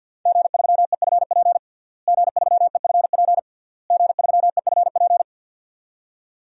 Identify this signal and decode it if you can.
Morse code, heard as a keyed tone, about 52 wpm, 700 Hz. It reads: G3EFP G3EFP G3EFP
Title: DXCC-100 Call Signs - 5 Characters - Repeated 3x - 50wpm